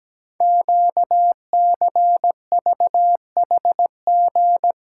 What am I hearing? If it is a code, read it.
Morse code, heard as a keyed tone, about 17 wpm, 700 Hz. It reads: QCVHG
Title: Code Groups - 5 Letters - 17wpm